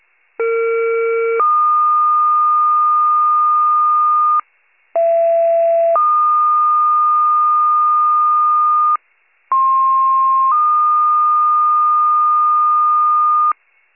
Two-tone page, heard as a one-second tone, then a three-second tone, about 460 then 1220 Hz, about 670 then 1220 Hz, about 1060 then 1220 Hz. It is heard from 0.4 to 13.5 s.